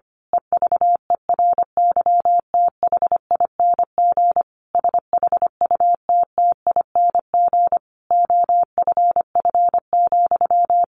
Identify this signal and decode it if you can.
Morse code, heard as a keyed tone, about 25 wpm, 700 Hz. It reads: EVERYTHING SHUTTING OFF,
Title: A Sound Of Thunder 25wpm